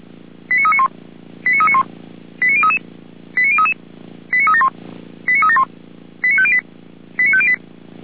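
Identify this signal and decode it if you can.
Five-tone selective call (ZVEI), heard as quick runs of five tones, tones of about 80 ms. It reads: 89391 89391 89033 89033 89371 89371 89508 89508
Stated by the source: ZVEI five-tone selective call 89050 received by SDR.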